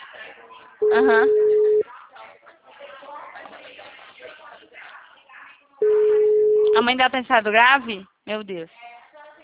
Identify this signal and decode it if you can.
Ringback tone (Europe, 425 Hz), with speech.